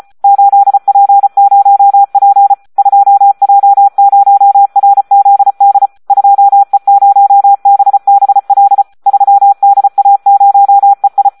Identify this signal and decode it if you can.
Morse code, heard as a keyed tone, about 34 wpm, 800 Hz. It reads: 8P0P 2J0RZD 2E0BBL 3DA0EI
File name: Audio practica CW / SD 1 34